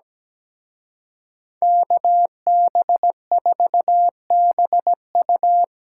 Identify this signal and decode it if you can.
Morse code, heard as a keyed tone, about 20 wpm, 700 Hz. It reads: KB4BU
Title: Speed Racing - USA 2x2 Call Signs 17wpm